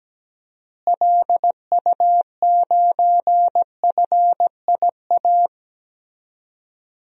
Morse code, heard as a keyed tone, about 17 wpm, 700 Hz. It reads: LU9FIA
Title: DXCC-100 Call Signs - 6 Characters - Repeated 3x - 17wpm